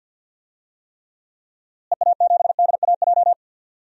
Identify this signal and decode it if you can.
Morse code, heard as a keyed tone, about 52 wpm, 700 Hz. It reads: EA7DAJ
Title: DXCC-100 Call Signs - 6 Characters - Repeated 2x - 50wpm